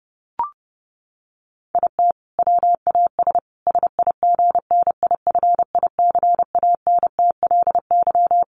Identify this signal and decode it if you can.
Morse code, heard as a keyed tone, about 30 wpm, 700 Hz. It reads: IT WAS SIGNIFICANTLY
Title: Sentences from Top 1500 Words 30wpm